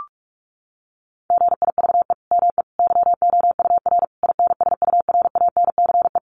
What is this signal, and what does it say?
Morse code, heard as a keyed tone, about 44 wpm, 700 Hz. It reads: GIVE ME YOUR INSURANCE
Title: Sentences from Top 1100 Words 45wpm